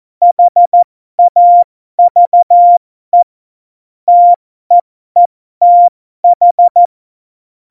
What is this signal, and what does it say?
Morse code, heard as a keyed tone, about 14 wpm, spaced as for 12 wpm, 700 Hz. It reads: HAVE TEETH
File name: Morsekurs / Witze